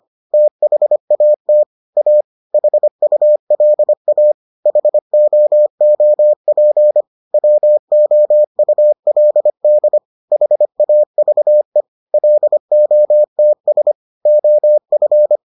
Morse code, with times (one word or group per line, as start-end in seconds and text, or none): 0.3-1.6 THAT
2.0-2.2 A
2.5-4.3 HULA
4.7-7.0 HOOP
7.3-10.0 WOULD
10.3-11.8 HAVE
12.1-13.9 LOTS
14.3-15.4 OF